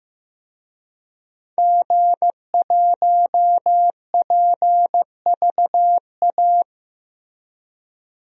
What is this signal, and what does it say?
Morse code, heard as a keyed tone, about 16 wpm, 700 Hz. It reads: G1PVA
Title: DXCC-100 Call Signs - 5 Characters - Repeated 2x - 15wpm